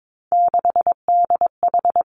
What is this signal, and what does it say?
Morse code, heard as a keyed tone, about 22 wpm, 700 Hz. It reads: 6DH
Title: Three Challenging Letter-Number Combinations 22wpm